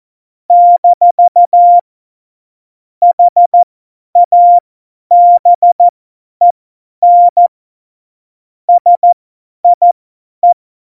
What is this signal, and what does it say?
Morse code, heard as a keyed tone, about 14 wpm, spaced as for 10 wpm, 700 Hz. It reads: - HABEN SIE
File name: Morsekurs / Witze / Witz 02 Juni 2020, 50 F70